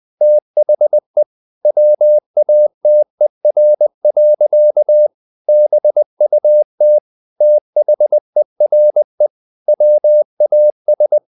Morse code, heard as a keyed tone, about 20 wpm, 600 Hz. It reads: THE WATER. BUT THERE WAS
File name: the old man and the sea part 2 20wpm